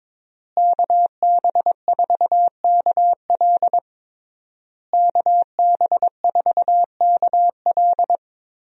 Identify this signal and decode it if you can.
Morse code, heard as a keyed tone, about 22 wpm, 700 Hz. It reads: KB4KL KB4KL
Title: USA Call Signs - 5 Characters - Repeated 3x - 22wpm